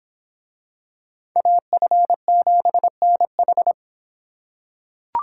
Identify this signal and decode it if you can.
Morse code, heard as a keyed tone, about 26 wpm, 700 Hz. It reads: AF7NH